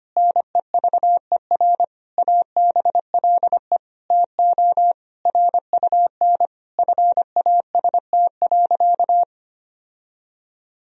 Morse code, heard as a keyed tone, about 25 wpm, 700 Hz. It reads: NEVER ABLE TO RUN FAST.
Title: Sentences from Top 1000 Words - Review Part 2 - 25wpm